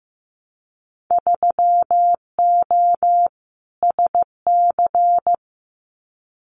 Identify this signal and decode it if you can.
Morse code, heard as a keyed tone, about 15 wpm, 700 Hz. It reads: 3O SC
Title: Winter Field Day Exchanges Encore 15wpm